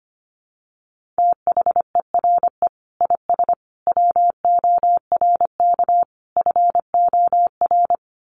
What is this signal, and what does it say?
Morse code, heard as a keyed tone, about 25 wpm, 700 Hz. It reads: THERE IS WORK FOR
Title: Sentences from Top 1400 Words 25wpm